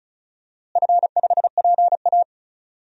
Morse code, heard as a keyed tone, about 36 wpm, 700 Hz. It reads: F5PA
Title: DXCC-100 Call Signs - 4 Characters - Repeated 2x - 35wpm